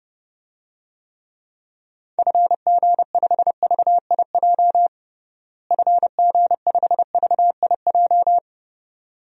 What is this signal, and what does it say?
Morse code, heard as a keyed tone, about 30 wpm, 700 Hz. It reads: FG5VIJ FG5VIJ
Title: DXCC-100 Call Signs - 6 Characters - Repeated 2x - 30wpm